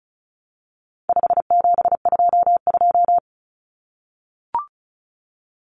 Morse code, heard as a keyed tone, about 34 wpm, 700 Hz. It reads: 5722
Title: Common Cut Numbers - 4-Digit 35wpm